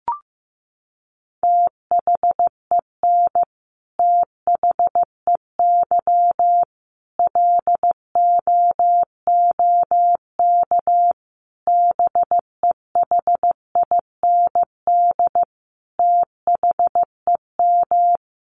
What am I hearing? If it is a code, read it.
Morse code, heard as a keyed tone, about 15 wpm, 700 Hz. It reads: THEN THEY LOOK BEHIND THEM
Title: Sentences from Top 1300 Words 15wpm